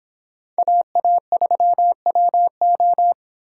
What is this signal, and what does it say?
Morse code, heard as a keyed tone, about 26 wpm, 700 Hz. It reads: AA3WO